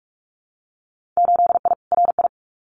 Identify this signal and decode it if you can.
Morse code, heard as a keyed tone, about 46 wpm, 700 Hz. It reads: 8I RI
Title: Winter Field Day Exchanges Encore 45wpm